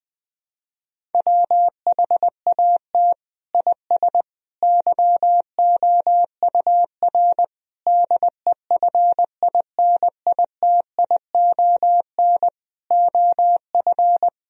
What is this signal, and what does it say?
Morse code, heard as a keyed tone, about 20 wpm, 700 Hz. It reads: WHAT IS YOUR DEFINITION OF
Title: Sentences from Top 1000 Words - Review Part 4 - 20wpm